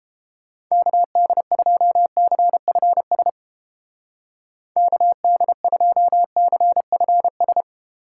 Morse code, heard as a keyed tone, about 29 wpm, 700 Hz. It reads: KD2CFS KD2CFS